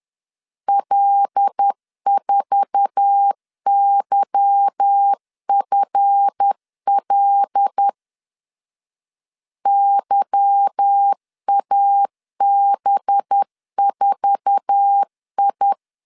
Morse code, heard as a keyed tone, about 10 wpm, 800 Hz. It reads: L4YFL YAB4I
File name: LEZIONI / 8 YJ45 / 8 YJ4 10wpm 2633